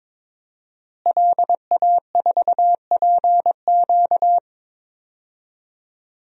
Morse code, heard as a keyed tone, about 22 wpm, 700 Hz. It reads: LA4PQ
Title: DXCC-100 Call Signs - 5 Characters - Repeated 2x - 22wpm